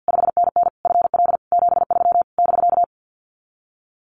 Morse code, heard as a keyed tone, about 48 wpm, 700 Hz. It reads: 5NN FL 73 <BK>
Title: Common QSO Phrases Encore 50wpm